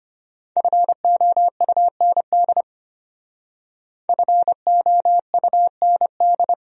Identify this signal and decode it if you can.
Morse code, heard as a keyed tone, about 32 wpm, 700 Hz. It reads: FOUND FOUND